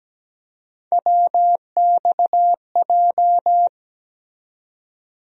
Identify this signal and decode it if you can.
Morse code, heard as a keyed tone, about 17 wpm, 700 Hz. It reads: WXJ
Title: USA Call Sign Postfixes - 3 Letters 17wpm